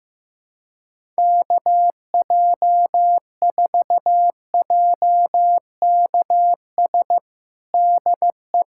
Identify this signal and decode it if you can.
Morse code, heard as a keyed tone, about 15 wpm, 700 Hz. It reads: KJ4JKS DE
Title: USA Callsign DE Callsign - 6 Characters - 15wpm